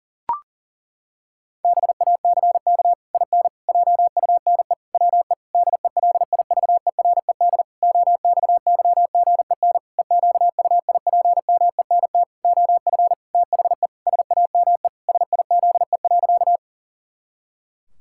Morse code, heard as a keyed tone, about 40 wpm, 700 Hz. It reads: BACK IN JUNE WE DELIVERED OXYGEN EQUIPMENT OF THE SAME SIZE.
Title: Pangram Challenge v2 - 40wpm